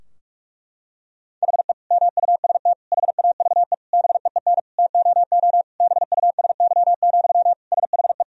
Morse code, heard as a keyed tone, about 46 wpm, 700 Hz. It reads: HE MUST HAVE BEEN TOO BUSY, SHE
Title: Sentences from Top 1000 Words - Review Part 3 - 45wpm